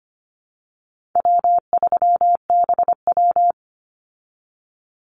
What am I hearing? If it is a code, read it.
Morse code, heard as a keyed tone, about 25 wpm, 700 Hz. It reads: W3BW
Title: Easy Call Signs Repeated 3x - 25wpm